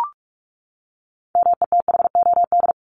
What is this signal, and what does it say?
Morse code, heard as a keyed tone, about 44 wpm, 700 Hz. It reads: METHOD
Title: ES:S Suffix - Top 100 Words with Base Words 45wpm